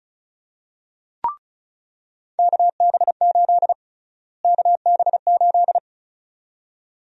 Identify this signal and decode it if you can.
Morse code, heard as a keyed tone, about 36 wpm, 700 Hz. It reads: KB8 KB8
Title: USA Call Sign Prefixes - Repeated 2x - 35wpm